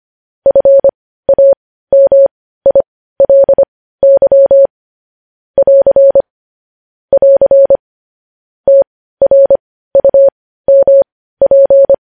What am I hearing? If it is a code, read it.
Morse code, heard as a keyed tone, about 25 wpm, spaced as for 15 wpm, 550 Hz. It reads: FAMILY <AR> <AR> TRUMP
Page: news headlines in morse code at 15 wpm